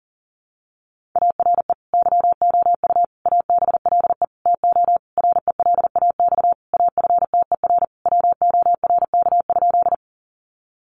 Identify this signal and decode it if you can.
Morse code, heard as a keyed tone, about 40 wpm, 700 Hz. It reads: ARE YOU ABLE TO RELAX AFTER WORK?
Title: Sentences from Top 1400 Words 40wpm